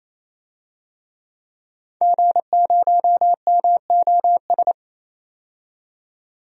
Morse code, heard as a keyed tone, about 28 wpm, 700 Hz. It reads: G0MOS
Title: DXCC-100 Call Signs - 5 Characters - 28wpm